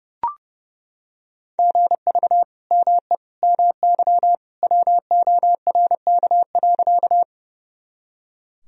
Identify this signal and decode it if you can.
Morse code, heard as a keyed tone, about 30 wpm, 700 Hz. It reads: GV ME MY WORK.
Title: Sentences from Top 200 Words and Abbreviations 30wpm